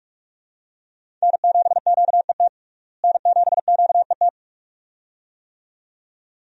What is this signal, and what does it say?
Morse code, heard as a keyed tone, about 42 wpm, 700 Hz. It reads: N7QET N7QET
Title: USA Call Signs - 5 Characters - Repeated 2x - 45wpm